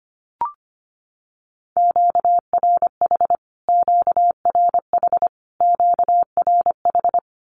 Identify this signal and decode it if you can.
Morse code, heard as a keyed tone, about 25 wpm, 700 Hz. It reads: QRH QRH QRH
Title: Q-Codes Repeated 3x 25wpm